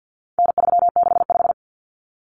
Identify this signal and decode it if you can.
Morse code, heard as a keyed tone, about 50 wpm, 700 Hz. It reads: N365
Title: Common Cut Numbers - 4-Digit 50wpm